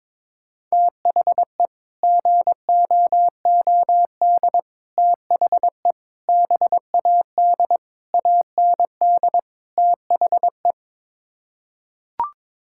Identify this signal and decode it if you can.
Morse code, heard as a keyed tone, about 22 wpm, 700 Hz. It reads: THE GOOD THE BAD AND THE
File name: practice / 22wpm / Sets of 6 Words - 22wpm